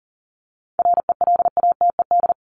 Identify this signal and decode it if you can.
Morse code, heard as a keyed tone, about 40 wpm, 700 Hz. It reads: RELATED
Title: ED Suffix - Top 100 Words with Base Words 40wpm